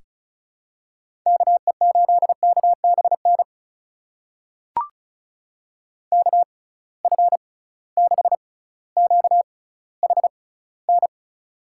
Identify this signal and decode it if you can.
Morse code, heard as a keyed tone, about 36 wpm, spaced as for 18 wpm, 700 Hz. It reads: KE8KBN KF6QHN
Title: USA Call Signs - 6 Characters - 6x Character Spacing - 35wpm